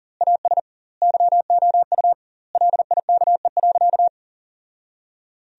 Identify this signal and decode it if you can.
Morse code, heard as a keyed tone, about 40 wpm, 700 Hz. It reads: AS YOU LIKE.
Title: 2x Word-Spacing - Sentences from Top 500 Words - 40wpm